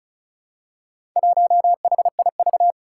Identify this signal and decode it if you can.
Morse code, heard as a keyed tone, about 36 wpm, 700 Hz. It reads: 1HIV
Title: Four Challenging Letter-Number Combinations 35wpm